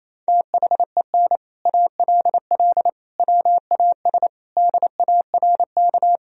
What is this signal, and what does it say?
Morse code, heard as a keyed tone, about 28 wpm, 700 Hz. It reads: THEN ALL WAS DARK
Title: Sentences from Top 500 Words and Abbreviations 28wpm